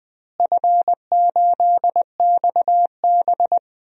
Morse code, heard as a keyed tone, about 20 wpm, 700 Hz. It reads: F8XB
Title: DXCC-100 Call Signs - 4 Characters - Repeated 3x - 20wpm